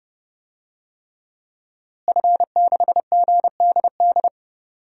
Morse code, heard as a keyed tone, about 31 wpm, 700 Hz. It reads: F6GDD
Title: DXCC-100 Call Signs - 5 Characters - Repeated 3x - 30wpm